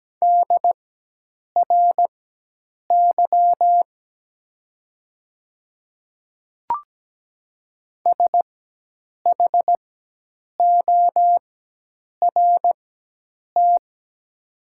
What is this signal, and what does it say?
Morse code, heard as a keyed tone, about 17 wpm, spaced as for 8 wpm, 700 Hz. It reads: DRY SHORT
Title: 4x Character Spacing - Top 500 Words - 17wpm